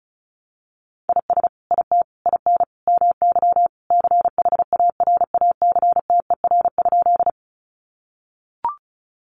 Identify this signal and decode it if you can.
Morse code, heard as a keyed tone, about 34 wpm, 700 Hz. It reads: IS IT IN MY CHARACTER?
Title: Sentences from Top 1200 Words 35wpm